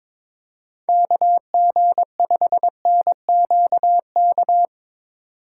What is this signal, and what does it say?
Morse code, heard as a keyed tone, about 22 wpm, 700 Hz. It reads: KG5NQK
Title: USA Call Signs - 6 Characters - 4x Character Spacing - 22wpm